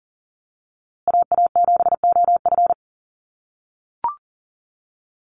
Morse code, heard as a keyed tone, about 40 wpm, 700 Hz. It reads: AA7OF
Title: USA Call Signs Repeated 3x 40wpm